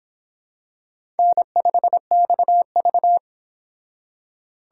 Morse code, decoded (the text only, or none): N5XV